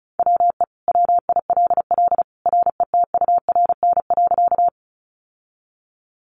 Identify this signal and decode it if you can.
Morse code, heard as a keyed tone, about 34 wpm, 700 Hz. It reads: WE WILL RETURN.
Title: Sentences from Top 800 Words 35wpm